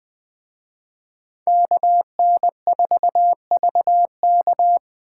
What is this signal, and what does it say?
Morse code, heard as a keyed tone, about 19 wpm, 700 Hz. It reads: KN4VK